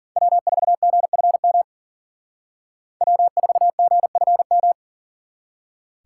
Morse code, heard as a keyed tone, about 52 wpm, 700 Hz. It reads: W4GFM W4GFM